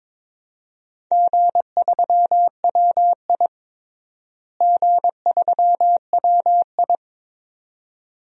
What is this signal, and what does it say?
Morse code, heard as a keyed tone, about 22 wpm, 700 Hz. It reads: G3WI G3WI
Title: DXCC-100 Call Signs - 4 Characters - Repeated 2x - 22wpm